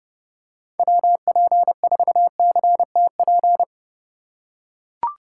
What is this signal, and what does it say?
Morse code, heard as a keyed tone, about 30 wpm, 700 Hz. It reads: WP4CTP
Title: USA Call Signs - 6 Characters - 2x Character Spacing - 30wpm